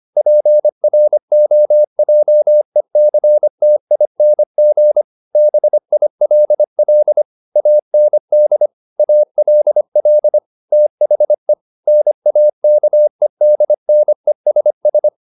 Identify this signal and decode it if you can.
Morse code, heard as a keyed tone, about 25 wpm, 600 Hz. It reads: PROJECTING BILL AND ALL THE NAKEDNESS